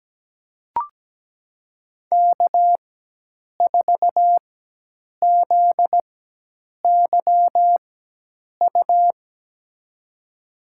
Morse code, heard as a keyed tone, about 17 wpm, spaced as for 8 wpm, 700 Hz. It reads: K4ZYU
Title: USA Call Signs - 5 Characters - 4x Character Spacing - 17wpm